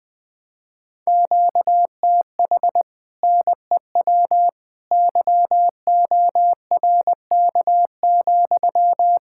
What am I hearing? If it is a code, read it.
Morse code, heard as a keyed tone, about 20 wpm, 700 Hz. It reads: QTH NEW YORK,
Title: Speed-Racing Common QSO Phrases - 17wpm